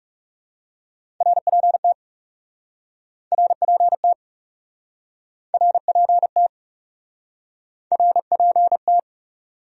Morse code, heard as a keyed tone, about 37 wpm, 700 Hz. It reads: RPT RPT RPT RPT
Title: Speed-Racing Abbreviations - 30wpm